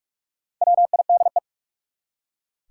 Morse code, heard as a keyed tone, about 44 wpm, 700 Hz. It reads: WIDE
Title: Four Letter Words - 45wpm